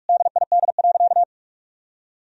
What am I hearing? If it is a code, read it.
Morse code, heard as a keyed tone, about 46 wpm, 700 Hz. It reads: DID.
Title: Sentences from Top 1000 Words - Review Part 4 - 45wpm